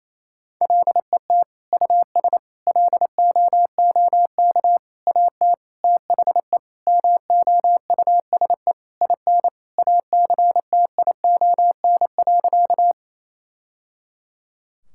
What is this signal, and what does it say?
Morse code, heard as a keyed tone, about 28 wpm, 700 Hz. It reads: LET US LOOK AT THE MOUSE IN ACTION.
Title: Sentences from Top 1000 Words - Review Part 5 - 28wpm